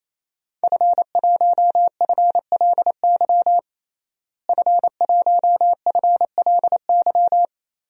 Morse code, heard as a keyed tone, about 28 wpm, 700 Hz. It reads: F1FLY F1FLY
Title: DXCC-100 Call Signs - 5 Characters - Repeated 3x - 28wpm